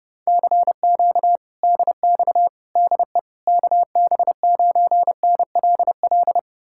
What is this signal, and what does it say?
Morse code, heard as a keyed tone, about 33 wpm, 700 Hz. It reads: CQ DX DE KB9NLL